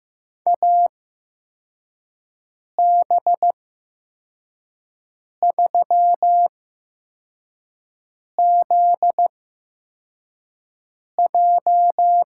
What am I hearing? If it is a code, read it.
Morse code, heard as a keyed tone, about 15 wpm, 700 Hz. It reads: A B 3 Z J
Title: USA Call Signs - 8x Character Spacing - 15wpm